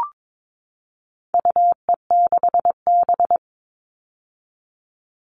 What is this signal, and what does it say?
Morse code, heard as a keyed tone, about 22 wpm, 700 Hz. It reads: UE6B
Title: All Cut Numbers - 4-Digit 22wpm